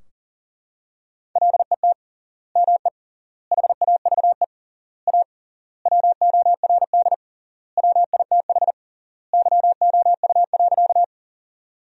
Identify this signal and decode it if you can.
Morse code, heard as a keyed tone, about 40 wpm, 700 Hz. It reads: LET ME HAVE A WORD WITH YOU.
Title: 3x Word-Spacing - Sentences from Top 300 Words - 40wpm